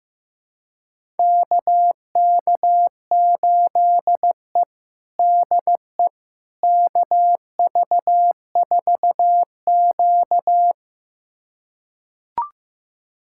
Morse code, heard as a keyed tone, about 15 wpm, 700 Hz. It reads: KK8E DE KV4Q
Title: USA Callsign DE Callsign - 4 Characters - 15wpm